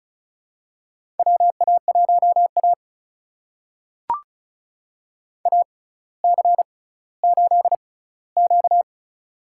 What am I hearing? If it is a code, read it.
Morse code, heard as a keyed tone, about 36 wpm, spaced as for 18 wpm, 700 Hz. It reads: WA1A AC8Q
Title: USA Call Signs - 4 Characters - 6x Character Spacing - 35wpm